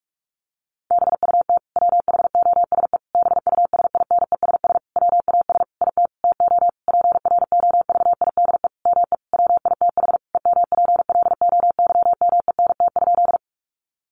Morse code, heard as a keyed tone, about 44 wpm, 700 Hz. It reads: BUT WHOSE BUSINESS WAS IT TO PROVIDE ME WITH EMPLOYMENT?
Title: Sentences from Top 1500 Words 45wpm